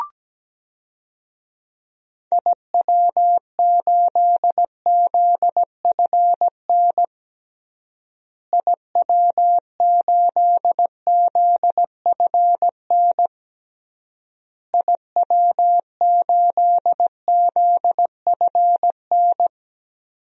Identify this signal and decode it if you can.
Morse code, heard as a keyed tone, about 17 wpm, 700 Hz. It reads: IW8ZFN IW8ZFN IW8ZFN
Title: DXCC-100 Call Signs - 6 Characters - Repeated 3x - 17wpm